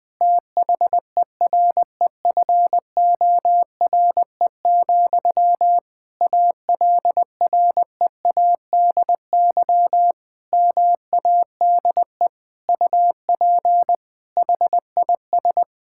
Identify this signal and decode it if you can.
Morse code, heard as a keyed tone, about 21 wpm, 700 Hz. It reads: THEREFORE, ALREADY MADE UP HIS
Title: Sentences from Top 1000 Words - Review Part 5 - 20wpm